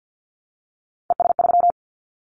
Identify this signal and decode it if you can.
Morse code, heard as a keyed tone, about 50 wpm, 700 Hz. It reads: ES3
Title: Callsign Prefixes for Top 50 DXCC - 50wpm